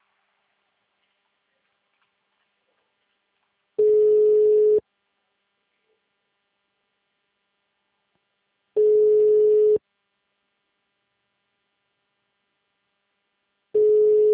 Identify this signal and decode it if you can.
Ringback tone (Europe, 425 Hz).